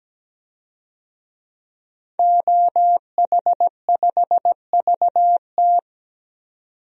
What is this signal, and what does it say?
Morse code, heard as a keyed tone, about 17 wpm, 700 Hz. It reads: OH5VT
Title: DXCC-100 Call Signs - 5 Characters - 17wpm